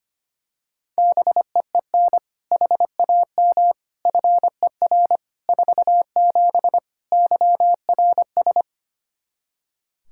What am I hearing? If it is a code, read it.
Morse code, heard as a keyed tone, about 25 wpm, 700 Hz. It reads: BEEN HAM FER 47 YRS
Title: Practice QSO - Exchange 3 - More Info - 25wpm